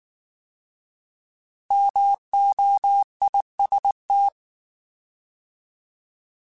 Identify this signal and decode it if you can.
Morse code, heard as a keyed tone, about 19 wpm, 800 Hz. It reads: MOIST
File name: Alphabetical / Alphabetical mp3s / Ms @18